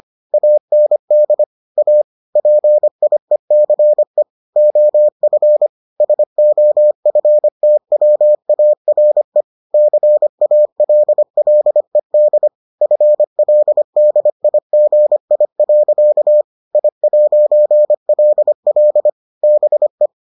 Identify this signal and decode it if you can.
Morse code, heard as a keyed tone, about 25 wpm, 600 Hz. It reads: AND A PIECE OF SOFTWARE CALLED FLDIGI. I'LL BE